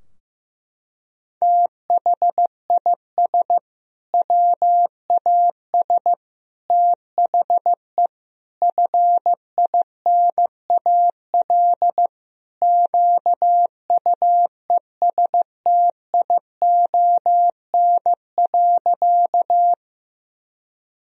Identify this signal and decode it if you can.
Morse code, heard as a keyed tone, about 15 wpm, 700 Hz. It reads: THIS WAS THE FINAL QUESTION.
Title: Sentences from Top 1000 Words - Review Part 2 - 15wpm